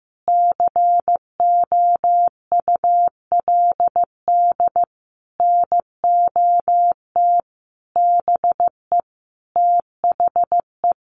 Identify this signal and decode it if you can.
Morse code, heard as a keyed tone, about 15 wpm, 700 Hz. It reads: COULD NOT BE THE
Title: Sentences from Top 1500 Words 15wpm